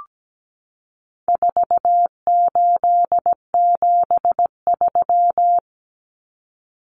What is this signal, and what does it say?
Morse code, heard as a keyed tone, about 17 wpm, 700 Hz. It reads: 4873
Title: Common Cut Numbers - 4-Digit 17wpm